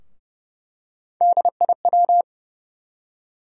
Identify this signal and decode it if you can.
Morse code, heard as a keyed tone, about 30 wpm, 700 Hz. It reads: DIW
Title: USA Call Sign Postfixes - 3 Letters 30wpm